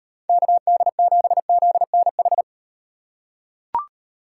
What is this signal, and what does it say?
Morse code, heard as a keyed tone, about 38 wpm, 700 Hz. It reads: KD7ZNH